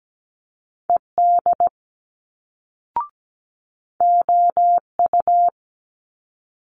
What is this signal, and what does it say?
Morse code, heard as a keyed tone, about 17 wpm, 700 Hz. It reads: ED OU
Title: Multi-letter Phonemes Quick Fire 17wpm